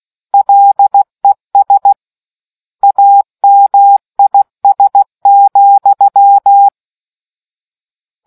Morse code, heard as a keyed tone, about 16 wpm, 800 Hz. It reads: LES AMIS,